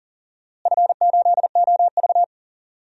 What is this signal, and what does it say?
Morse code, heard as a keyed tone, about 40 wpm, 700 Hz. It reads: F8OV
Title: DXCC-100 Call Signs - 4 Characters - Repeated 3x - 40wpm